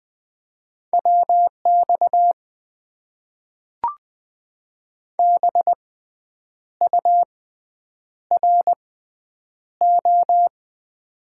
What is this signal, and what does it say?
Morse code, heard as a keyed tone, about 20 wpm, spaced as for 8 wpm, 700 Hz. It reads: WX BURO
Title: Top 90 QSO Elements - 6x Character Spacing - 20wpm